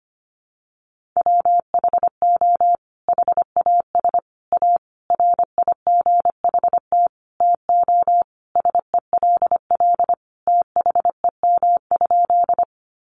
Morse code, heard as a keyed tone, about 25 wpm, 700 Hz. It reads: WHO HAS A RIGHT TO SELL THEM?
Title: Sentences from Top 1100 Words 25wpm